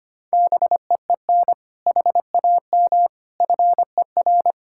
Morse code, heard as a keyed tone, about 25 wpm, 700 Hz. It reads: BEEN HAM FER
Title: Practice QSO - Exchange 3 - More Info - 25wpm